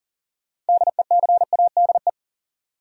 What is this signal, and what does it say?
Morse code, heard as a keyed tone, about 40 wpm, 700 Hz. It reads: DECADE
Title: Sets of 5 Words - New Word Drill - 40wpm